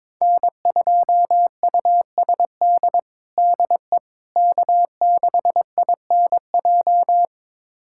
Morse code, heard as a keyed tone, about 22 wpm, 700 Hz. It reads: N2USD DE K6INJ
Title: USA Callsign DE Callsign - 5 Characters - 22wpm